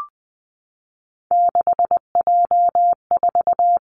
Morse code, heard as a keyed tone, about 20 wpm, 700 Hz. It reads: 6J4
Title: Three Challenging Letter-Number Combinations 20wpm